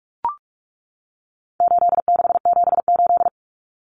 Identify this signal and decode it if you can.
Morse code, heard as a keyed tone, about 44 wpm, 700 Hz. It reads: 8678
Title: Numbers - 4-Digit 45wpm